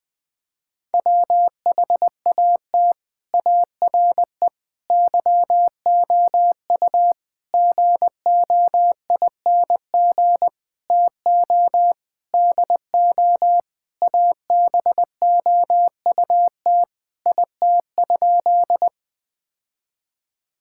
Morse code, heard as a keyed tone, about 19 wpm, 700 Hz. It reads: WHAT ARE YOU GOING TO DO ABOUT IT?
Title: Sentences with 6-Word Phrases - 20wpm